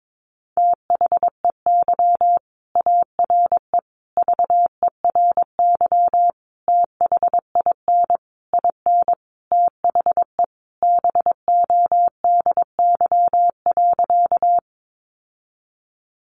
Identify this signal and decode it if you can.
Morse code, heard as a keyed tone, about 22 wpm, 700 Hz. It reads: THEY ARE VERY THIN IN THE BODY.
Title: Sentences from Top 1200 Words 22wpm